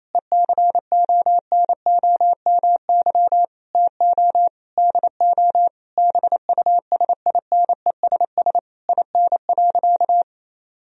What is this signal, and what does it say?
Morse code, heard as a keyed tone, about 28 wpm, 700 Hz. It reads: ECONOMY TO DO BUSINESS IN.
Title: Sentences from Top 1000 Words - Review Part 3 - 28wpm